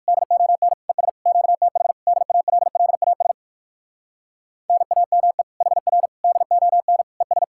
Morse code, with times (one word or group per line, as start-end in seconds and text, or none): 0.1-0.7 DON
0.9-1.1 ES
1.3-1.9 QTH
2.1-3.3 DALLAS
4.7-5.4 NAME
5.6-6.1 HR
6.2-7.0 DON
7.2-7.4 ES